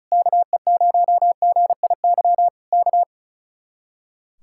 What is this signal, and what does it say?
Morse code, heard as a keyed tone, about 33 wpm, 700 Hz. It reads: KE0GIY K
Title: Speed-Racing Common QSO Phrases - 35wpm